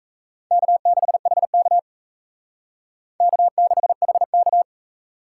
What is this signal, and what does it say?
Morse code, heard as a keyed tone, about 32 wpm, 700 Hz. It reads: K6HK K6HK
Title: Speed Racing - USA Call Signs 28wpm